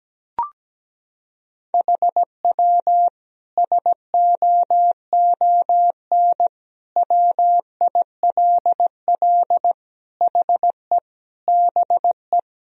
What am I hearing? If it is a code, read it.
Morse code, heard as a keyed tone, about 17 wpm, 700 Hz. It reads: HW SOON WILL HE BE
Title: Sentences from Top 400 Words and Abbreviations 17wpm